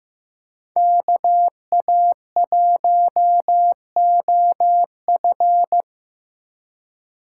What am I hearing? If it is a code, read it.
Morse code, heard as a keyed tone, about 17 wpm, 700 Hz. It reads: KA1OF
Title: Speed Racing - USA 2x2 Call Signs 15wpm